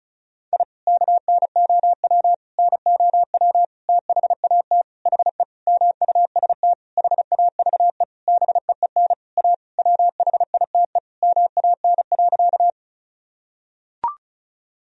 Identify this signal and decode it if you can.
Morse code, heard as a keyed tone, about 34 wpm, 690 Hz. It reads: I KNOW NOW THAT HE MUST HAVE BEEN A WHITE MAN.
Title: Sentences from Top 1000 Words - Review Part 1 - 35wpm